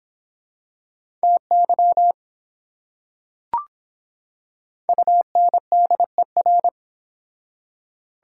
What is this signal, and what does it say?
Morse code, heard as a keyed tone, about 25 wpm, 700 Hz. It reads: TY UNDER